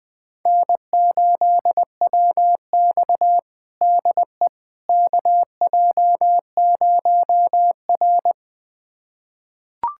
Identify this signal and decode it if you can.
Morse code, heard as a keyed tone, about 20 wpm, 700 Hz. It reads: N8WX DE KJ0R
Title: USA Callsign DE Callsign - 4 Characters - 20wpm